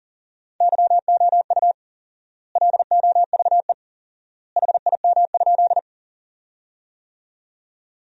Morse code, heard as a keyed tone, about 40 wpm, 700 Hz. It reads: YOU LOVE HIM?
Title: 4x Word-Spacing - Sentences from Top 400 Words - 40wpm